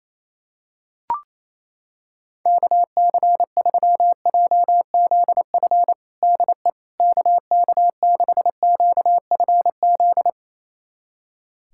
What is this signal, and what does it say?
Morse code, heard as a keyed tone, about 28 wpm, 700 Hz. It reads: KC3JZF DE KK6QFZ
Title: USA Callsign DE Callsign 28wpm